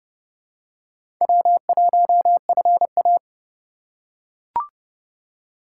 Morse code, heard as a keyed tone, about 30 wpm, 700 Hz. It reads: W1FA
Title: USA Call Signs - 4 Characters - 3x Character Spacing - 30wpm